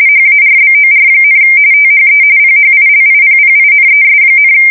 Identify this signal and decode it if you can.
RTTY, about 45.45 baud. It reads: HARBOR, THE PHILIPPINES,
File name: fsk170 / battleshipnj